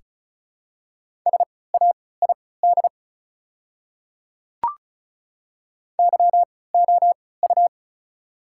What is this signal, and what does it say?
Morse code, heard as a keyed tone, about 34 wpm, spaced as for 20 wpm, 700 Hz. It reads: SAID YOU
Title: 3x Character Spacing - Top 300 Words - 35wpm